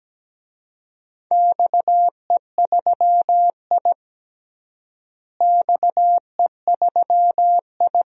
Morse code, heard as a keyed tone, about 17 wpm, 700 Hz. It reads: XE3I XE3I
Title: DXCC-100 Call Signs - 4 Characters - Repeated 3x - 17wpm